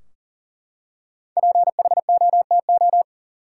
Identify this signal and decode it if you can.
Morse code, heard as a keyed tone, about 38 wpm, 700 Hz. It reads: PHOTO
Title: Five Letter Words - 40wpm